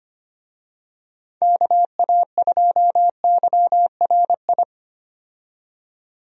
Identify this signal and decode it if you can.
Morse code, heard as a keyed tone, about 25 wpm, 700 Hz. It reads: KA2YRI
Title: USA Call Signs - 6 Characters - 25wpm